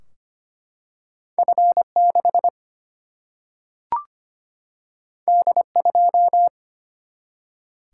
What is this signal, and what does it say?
Morse code, heard as a keyed tone, about 25 wpm, 700 Hz. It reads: F6 D2
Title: Single-Letter Single-Number 25wpm